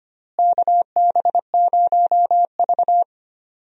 Morse code, heard as a keyed tone, about 25 wpm, 700 Hz. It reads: KB0V
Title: USA Call Signs - 4 Characters - Repeated 3x - 25wpm